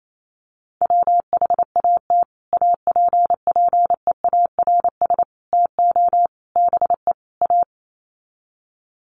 Morse code, heard as a keyed tone, about 28 wpm, 700 Hz. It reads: WHAT APPEARS TO BE A
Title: Sets of 5 Words Encore2 28wpm